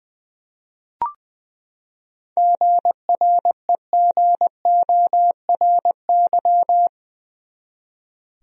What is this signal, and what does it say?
Morse code, heard as a keyed tone, about 20 wpm, 700 Hz. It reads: GREGORY
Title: USA Common Names - 20wpm